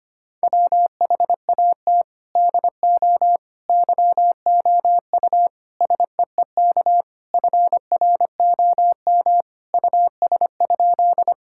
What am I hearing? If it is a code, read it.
Morse code, heard as a keyed tone, about 25 wpm, 700 Hz. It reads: WHAT DO YOU SEEK FROM US?
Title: Sentences from Top 1600 Words 25wpm